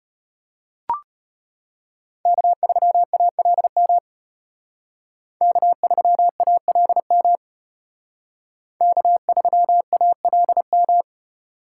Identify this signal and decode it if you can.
Morse code, heard as a keyed tone, about 33 wpm, 700 Hz. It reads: K3ALM K3ALM K3ALM